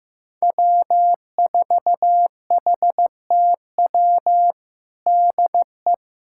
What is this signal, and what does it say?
Morse code, heard as a keyed tone, about 15 wpm, 700 Hz. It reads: W4HTW DE
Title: USA Callsign DE Callsign - 5 Characters - 15wpm